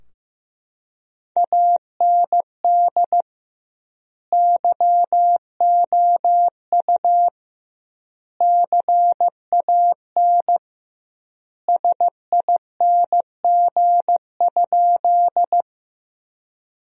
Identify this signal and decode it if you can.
Morse code, heard as a keyed tone, about 16 wpm, 700 Hz. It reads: AND YOU CAN SING?
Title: 2x Word-Spacing - Sentences from Top 400 Words - 15wpm